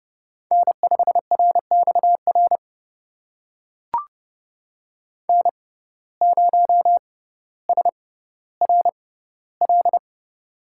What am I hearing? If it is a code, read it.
Morse code, heard as a keyed tone, about 30 wpm, spaced as for 15 wpm, 700 Hz. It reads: N5RXR N0SRL
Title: USA Call Signs - 5 Characters - 6x Character Spacing - 30wpm